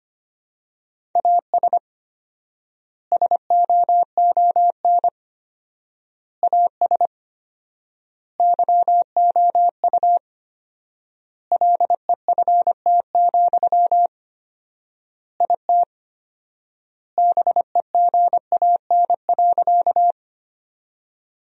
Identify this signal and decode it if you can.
Morse code, heard as a keyed tone, about 25 wpm, 700 Hz. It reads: AS SOON AS YOU LEFT, IT BEGAN.
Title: 4x Word-Spacing - Sentences from Top 400 Words - 25wpm